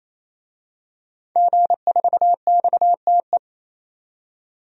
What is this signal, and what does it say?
Morse code, heard as a keyed tone, about 28 wpm, 700 Hz. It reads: G4XTE
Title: DXCC-100 Call Signs - 5 Characters - 28wpm